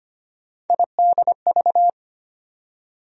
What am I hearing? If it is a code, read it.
Morse code, heard as a keyed tone, about 25 wpm, 700 Hz. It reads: IDV
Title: USA Call Sign Postfixes - 3 Letters 25wpm